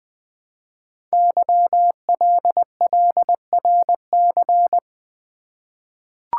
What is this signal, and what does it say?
Morse code, heard as a keyed tone, about 20 wpm, 700 Hz. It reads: YLLRC
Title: Code Groups - 5 Letters - 20wpm